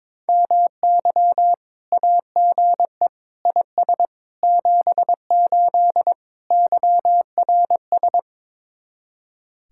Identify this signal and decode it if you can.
Morse code, heard as a keyed tone, about 22 wpm, 700 Hz. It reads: MY AGE IS 78 YRS
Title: Speed-Racing Common QSO Phrases - 22wpm